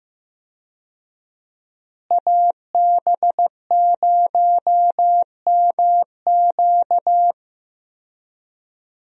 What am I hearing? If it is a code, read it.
Morse code, heard as a keyed tone, about 15 wpm, 700 Hz. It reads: AB0MQ
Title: USA Call Signs - 5 Characters - Repeated 2x - 15wpm